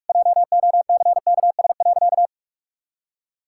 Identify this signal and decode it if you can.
Morse code, heard as a keyed tone, about 46 wpm, 700 Hz. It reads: JOCKS.
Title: Pangram Challenge v2 - 45wpm